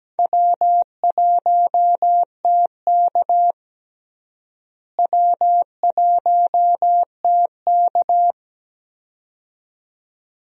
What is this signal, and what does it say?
Morse code, heard as a keyed tone, about 17 wpm, 700 Hz. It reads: W1TK W1TK
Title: USA Call Signs - 4 Characters - Repeated 2x - 17wpm